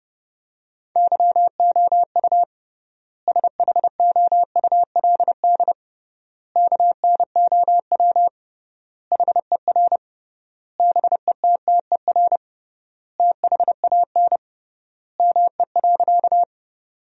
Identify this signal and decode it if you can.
Morse code, heard as a keyed tone, about 32 wpm, 700 Hz. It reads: YOU SHOULD KNOW HER BETTER THAN ME.
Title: 3x Word-Spacing - Sentences from Top 400 Words - 30wpm